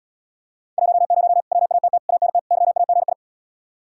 Morse code, heard as a keyed tone, about 80 wpm, 700 Hz. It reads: 10 WATT ANT YAGI